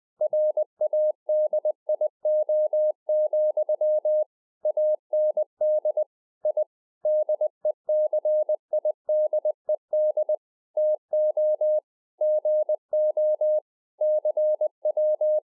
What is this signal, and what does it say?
Morse code, heard as a keyed tone, about 20 wpm, 610 Hz. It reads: RADIO, AND I DECIDED TO GO CW